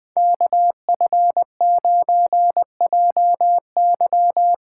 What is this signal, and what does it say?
Morse code, heard as a keyed tone, about 23 wpm, 700 Hz. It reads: KF9JY
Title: Speed Racing - USA 2x2 Call Signs 20wpm